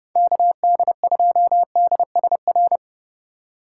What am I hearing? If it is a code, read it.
Morse code, heard as a keyed tone, about 32 wpm, 700 Hz. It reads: KD2DSR
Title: USA Call Signs - 6 Characters - 3x Character Spacing - 30wpm